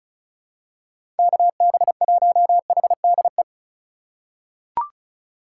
Morse code, heard as a keyed tone, about 34 wpm, 700 Hz. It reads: KB1HDE
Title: USA Call Signs - 6 Characters - 4x Character Spacing - 35wpm